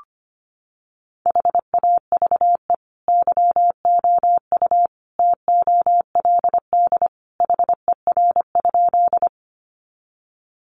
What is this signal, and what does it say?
Morse code, heard as a keyed tone, about 25 wpm, 700 Hz. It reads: HAVE YOU TOLD HER?
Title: Sentences from Top 400 Words 25wpm